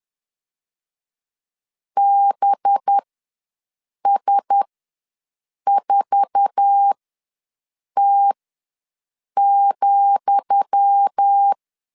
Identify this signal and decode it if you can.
Morse code, heard as a keyed tone, about 10 wpm, spaced as for 6 wpm, 800 Hz. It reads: BS4T,
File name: LEZIONI / 10 SIGNS / 10D COMA ws 10wpm 4314